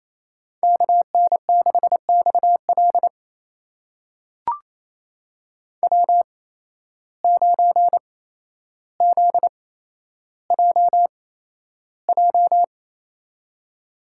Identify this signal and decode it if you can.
Morse code, heard as a keyed tone, about 28 wpm, spaced as for 12 wpm, 700 Hz. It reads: KN6XL W9ZJJ
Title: USA Call Signs - 5 Characters - 8x Character Spacing - 28wpm